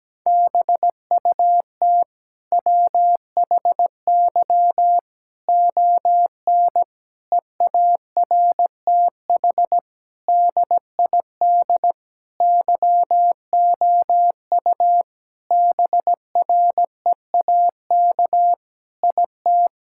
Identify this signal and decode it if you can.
Morse code, heard as a keyed tone, about 17 wpm, 700 Hz. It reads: BUT WHY ON EARTH DID YOU BREAK IT
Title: Sentences from Top 1000 Words - Review Part 6 - 17wpm